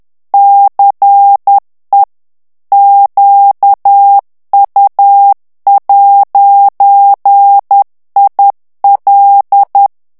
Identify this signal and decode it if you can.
Morse code, heard as a keyed tone, about 10 wpm, 800 Hz. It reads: CE QU'IL
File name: FRANCAIS TEXTE CLAIR 2 VITESSE 10 MOTS MINUTES